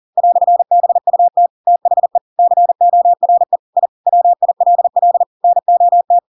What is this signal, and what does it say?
Morse code, heard as a keyed tone, about 40 wpm, 700 Hz. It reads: "BUT THE CORE I WILL NOT
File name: Pinocchio / Pinocchio ch7 40 40 ik1hge